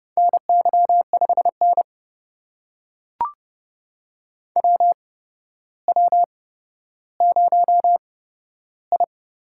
Morse code, heard as a keyed tone, about 30 wpm, spaced as for 13 wpm, 700 Hz. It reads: NY5N WW0I
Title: USA Call Signs - 4 Characters - 8x Character Spacing - 30wpm